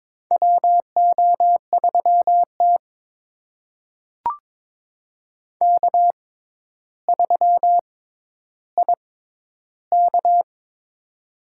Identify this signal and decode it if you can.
Morse code, heard as a keyed tone, about 22 wpm, spaced as for 11 wpm, 700 Hz. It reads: WO3T K3IK
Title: USA Call Signs - 4 Characters - 6x Character Spacing - 22wpm